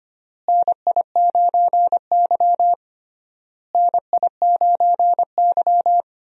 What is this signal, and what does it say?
Morse code, heard as a keyed tone, about 25 wpm, 700 Hz. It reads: NI9Y NI9Y
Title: USA Call Signs - 4 Characters - Repeated 3x - 25wpm